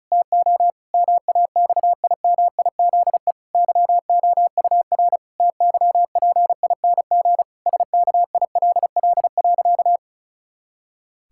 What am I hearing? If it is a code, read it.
Morse code, heard as a keyed tone, about 36 wpm, 700 Hz. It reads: TO MAXIMIZE YOUR TYPING SKILL.
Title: Pangram Challenge v2 - 35wpm